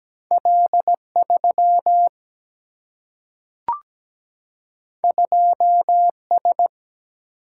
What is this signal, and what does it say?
Morse code, heard as a keyed tone, about 17 wpm, 700 Hz. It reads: L 3 2 S
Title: Single-Letter Single-Number 17wpm